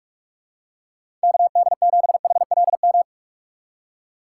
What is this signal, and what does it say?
Morse code, heard as a keyed tone, about 44 wpm, 700 Hz. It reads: KD7HLM
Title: USA Call Signs - 6 Characters - 4x Character Spacing - 45wpm